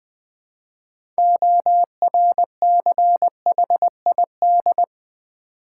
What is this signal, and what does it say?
Morse code, heard as a keyed tone, about 20 wpm, 690 Hz. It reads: ORCHID
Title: Easy Words with Hints - 20wpm